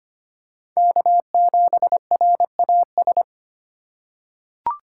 Morse code, heard as a keyed tone, about 25 wpm, 700 Hz. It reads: K7RAS
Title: USA Call Signs - 5 Characters - 2x Character Spacing - 25wpm